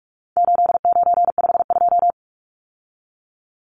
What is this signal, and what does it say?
Morse code, heard as a keyed tone, about 44 wpm, 700 Hz. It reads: 8952
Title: Numbers - 4-Digit 45wpm